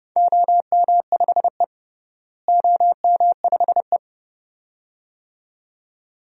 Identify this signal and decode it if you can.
Morse code, heard as a keyed tone, about 30 wpm, 700 Hz. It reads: OM5E OM5E
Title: DXCC-100 Call Signs - 4 Characters - Repeated 2x - 30wpm